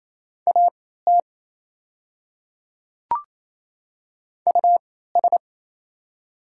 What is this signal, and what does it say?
Morse code, heard as a keyed tone, about 27 wpm, spaced as for 16 wpm, 700 Hz. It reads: AT US
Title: 3x Character Spacing - Top 300 Words - 28wpm